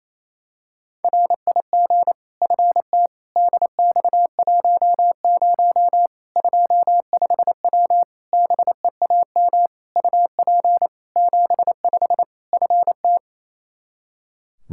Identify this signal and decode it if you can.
Morse code, heard as a keyed tone, about 28 wpm, 700 Hz. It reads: RIG FT DX10 25W BEAM UP 75 FT
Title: Practice QSO - Exchange 2 - Info - 28wpm